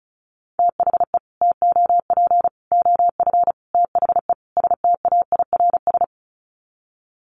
Morse code, heard as a keyed tone, about 34 wpm, 700 Hz. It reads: THE TOP OF THE STAIRS
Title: Sets of 5 Words Encore 35wpm